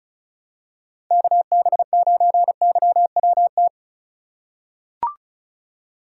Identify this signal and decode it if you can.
Morse code, heard as a keyed tone, about 36 wpm, 700 Hz. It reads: KB9YWT
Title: USA Call Signs - 2x Character Spacing - 35wpm